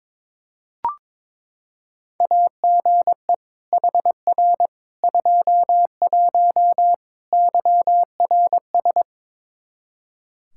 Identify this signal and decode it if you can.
Morse code, heard as a keyed tone, about 22 wpm, 700 Hz. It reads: AGE HR 21 YRS
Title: K1USN SST QSO Practice 22wpm